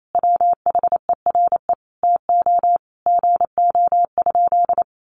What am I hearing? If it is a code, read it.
Morse code, heard as a keyed tone, about 28 wpm, 700 Hz. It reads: WHERE TO GO?
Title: Sentences from Top 1300 Words 28wpm